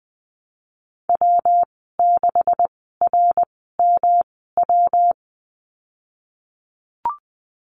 Morse code, heard as a keyed tone, about 20 wpm, spaced as for 15 wpm, 700 Hz. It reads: W6RMW
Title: 173 Lesson 28g - Number 9 - Call Signs 20wpm